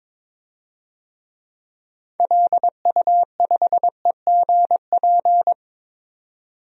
Morse code, heard as a keyed tone, about 22 wpm, 700 Hz. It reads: LU5EGP
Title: DXCC-100 Call Signs - 6 Characters - 22wpm